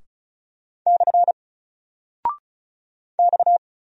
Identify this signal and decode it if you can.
Morse code, heard as a keyed tone, about 36 wpm, 700 Hz. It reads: / X
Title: Single Character (Letter:Number:Punctuation:Prosign) 35wpm